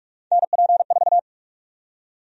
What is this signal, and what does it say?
Morse code, heard as a keyed tone, about 44 wpm, 700 Hz. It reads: NP4
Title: USA Call Sign Prefixes - Repeated 2x - 45wpm